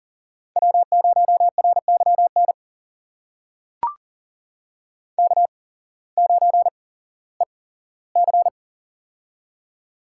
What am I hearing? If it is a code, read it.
Morse code, heard as a keyed tone, about 40 wpm, spaced as for 19 wpm, 700 Hz. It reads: W0RYN K9EC
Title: USA Call Signs - 8x Character Spacing - 40wpm